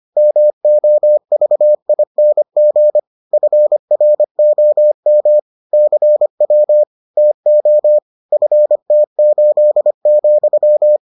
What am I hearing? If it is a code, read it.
Morse code, heard as a keyed tone, about 25 wpm, 600 Hz. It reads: MOVING FROM CW TO FT8,